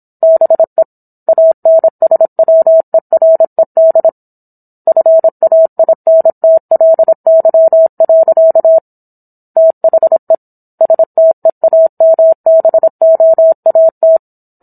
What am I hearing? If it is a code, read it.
Morse code, heard as a keyed tone, about 27 wpm, 650 Hz. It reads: BE ANSWERED FAINTLY. THE STEAMBOAT
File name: Book One / 17-Chapter 17 - 26 WPM